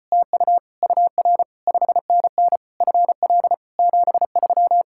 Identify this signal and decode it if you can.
Morse code, heard as a keyed tone, about 33 wpm, 700 Hz. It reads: TU UR 5NN FL 73